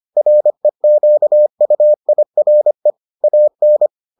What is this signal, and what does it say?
Morse code, heard as a keyed tone, about 25 wpm, 600 Hz. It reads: REQUIRE AN